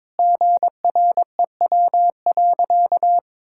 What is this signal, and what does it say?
Morse code, heard as a keyed tone, about 22 wpm, 700 Hz. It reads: GREW.
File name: practice / 22wpm / Sentences with 8-Word Phrases - 22wpm